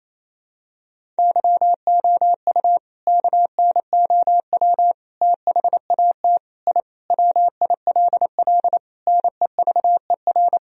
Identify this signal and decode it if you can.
Morse code, heard as a keyed tone, about 28 wpm, 700 Hz. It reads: YOU KNOW THAT I WILL NEVER
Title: Sentences from Top 1000 Words - Review Part 1 - 28wpm